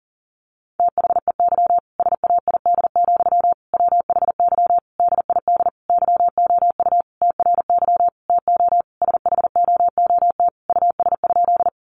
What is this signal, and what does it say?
Morse code, heard as a keyed tone, about 40 wpm, 700 Hz. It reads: THEY SAID, WHY DID YOU TRY TO SHOOT US?
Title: Sentences from Top 1200 Words 40wpm